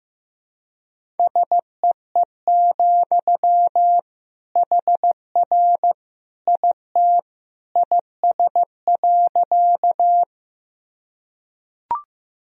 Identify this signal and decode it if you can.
Morse code, heard as a keyed tone, about 15 wpm, 700 Hz. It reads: SEE, HR IT IS.
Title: Sentences from Top 200 Words and Abbreviations 15wpm